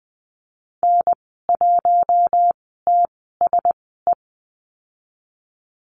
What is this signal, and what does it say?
Morse code, heard as a keyed tone, about 20 wpm, spaced as for 15 wpm, 700 Hz. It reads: N1TSE
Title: 082 Lesson 14g - Number 2 - Call Signs 20wpm